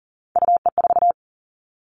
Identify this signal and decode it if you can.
Morse code, heard as a keyed tone, about 41 wpm, 700 Hz. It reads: UE4
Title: Three Challenging Letter-Number Combinations 40wpm